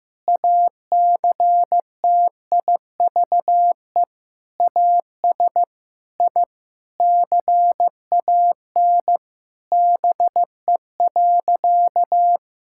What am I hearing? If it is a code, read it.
Morse code, heard as a keyed tone, about 15 wpm, 700 Hz. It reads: ACTIVE AS I CAN BE.
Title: Sentences from Top 1000 Words - Review Part 6 - 15wpm